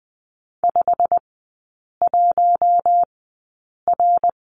Morse code, heard as a keyed tone, about 20 wpm, spaced as for 8 wpm, 700 Hz. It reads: 51R
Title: 093 Lesson 16e - Letter c - Three Characters 20wpm